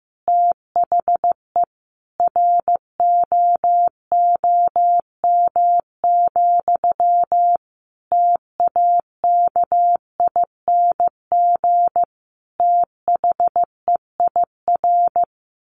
Morse code, heard as a keyed tone, about 15 wpm, 700 Hz. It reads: THE ROOM, TAKING THEIR
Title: A Sound Of Thunder 15wpm